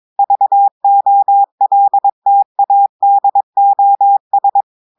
Morse code, heard as a keyed tone, about 22 wpm, 800 Hz. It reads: VOLTADOS
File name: audios / portugais / portugais texte en clair 2 vitesse 22 mots minutes